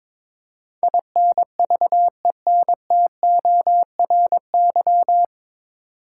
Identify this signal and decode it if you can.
Morse code, heard as a keyed tone, about 22 wpm, 700 Hz. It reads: INVENTORY
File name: 9 Letter Words - First 3 Letters Given - 22wpm